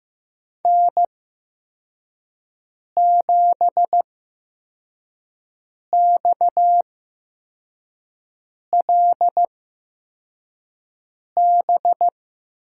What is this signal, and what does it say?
Morse code, heard as a keyed tone, about 15 wpm, spaced as for 4 wpm, 700 Hz. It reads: N7XLB
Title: USA Call Signs - 5 Characters - 8x Character Spacing - 15wpm